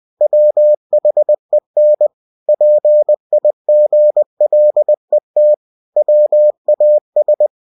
Morse code, heard as a keyed tone, about 20 wpm, 600 Hz. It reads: WHEN PIGLET WAS
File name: cw-books / winnie-the-pooh / winnie-20wpm-page-0021